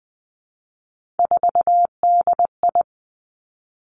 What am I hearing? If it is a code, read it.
Morse code, heard as a keyed tone, about 20 wpm, 700 Hz. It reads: 4DI
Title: Three Challenging Letter-Number Combinations 20wpm